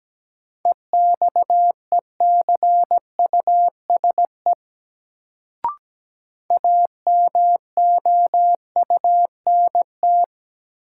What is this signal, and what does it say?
Morse code, heard as a keyed tone, about 17 wpm, 700 Hz. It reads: EXECUSE AMOUNT
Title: Sentences from Top 700 Words - New Word Drill - 17wpm